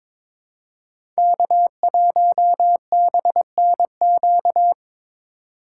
Morse code, heard as a keyed tone, about 22 wpm, 700 Hz. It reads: K1BNQ
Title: USA Call Signs - 3x Character Spacing - 22wpm